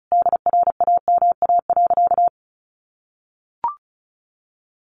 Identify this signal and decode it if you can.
Morse code, heard as a keyed tone, about 34 wpm, 700 Hz. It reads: DRAMA.
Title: Sentences from Top 1500 Words 35wpm